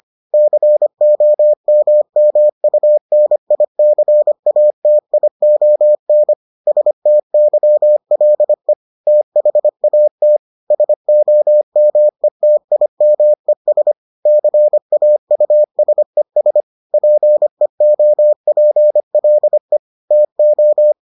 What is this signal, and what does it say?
Morse code, heard as a keyed tone, about 25 wpm, 600 Hz. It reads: COMMUNICATION STYLE THAT SOMETIMES CAUSES PEOPLE TO